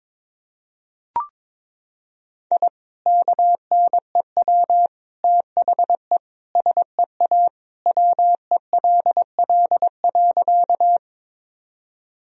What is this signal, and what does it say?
Morse code, heard as a keyed tone, about 22 wpm, 700 Hz. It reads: I KNEW THE SEA WELL.
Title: Sentences from Top 1000 Words - Review Part 1 - 22wpm